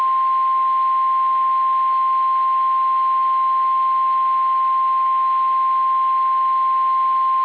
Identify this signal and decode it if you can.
Emergency Alert System (SAME) tones (the 1050 Hz attention tone). It is heard from 0.6 to 7.2 s.
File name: eas / KSDZKDJL / KSDZKDJL